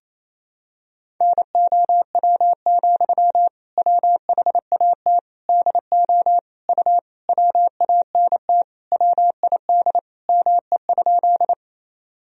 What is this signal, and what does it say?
Morse code, heard as a keyed tone, about 27 wpm, 700 Hz. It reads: NOW, WHAT DO U WANT WID ME?
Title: Sentences from Top 200 Words and Abbreviations 28wpm